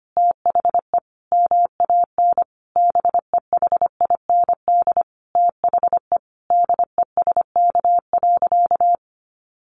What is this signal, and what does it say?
Morse code, heard as a keyed tone, about 25 wpm, 700 Hz. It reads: THE MAN BEHIND THE DESK.
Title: A Sound Of Thunder 25wpm